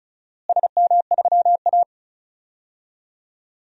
Morse code, heard as a keyed tone, about 34 wpm, 700 Hz. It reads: SM3A
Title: DXCC-100 Call Signs - 4 Characters - Repeated 3x - 35wpm